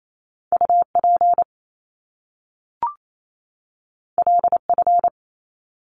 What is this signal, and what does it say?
Morse code, heard as a keyed tone, about 28 wpm, 700 Hz. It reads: UP LF
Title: Top Two Letter Combinations in Words 28wpm